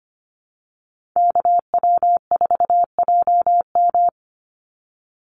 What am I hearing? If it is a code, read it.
Morse code, heard as a keyed tone, about 25 wpm, 700 Hz. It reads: KW4JM
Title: NA SOTA Activator Callsigns 25wpm